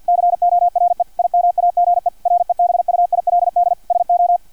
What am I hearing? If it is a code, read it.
Morse code, heard as a keyed tone, about 50 wpm, 700 Hz, with radio noise. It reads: CORE IMAGE REBUILD SO
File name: archived / CRSnet